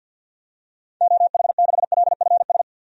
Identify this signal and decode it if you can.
Morse code, heard as a keyed tone, about 52 wpm, 700 Hz. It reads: OH6LFS
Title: DXCC-100 Call Signs - 6 Characters - 50wpm